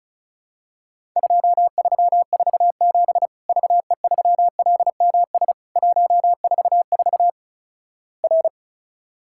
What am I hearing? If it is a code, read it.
Morse code, heard as a keyed tone, about 34 wpm, 700 Hz. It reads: 2347 VE3LMS 144 R
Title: Worked All Europe DX Contest - QTC Exchange - 35wpm